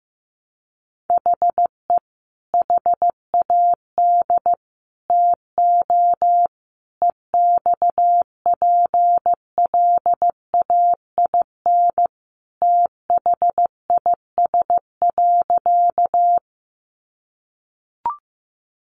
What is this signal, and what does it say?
Morse code, heard as a keyed tone, about 15 wpm, 700 Hz. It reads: HE HAD TO EXPLAIN THIS.
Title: Sentences from Top 1100 Words 15wpm